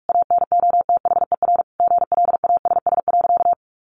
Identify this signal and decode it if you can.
Morse code, heard as a keyed tone, about 44 wpm, 700 Hz. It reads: ANOTHER GLASS.
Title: Sentences from Top 1500 Words 45wpm